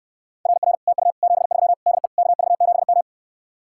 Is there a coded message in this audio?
Yes: Morse code from 0.5 to 3.0 s, heard as a keyed tone, about 69 wpm, 700 Hz. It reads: RR TU 73 DE KV7M